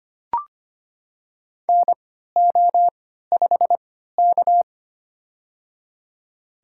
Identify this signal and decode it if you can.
Morse code, heard as a keyed tone, about 25 wpm, spaced as for 14 wpm, 700 Hz. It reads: NO5K
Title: USA Call Signs - 4 Characters - 3x Character Spacing - 25wpm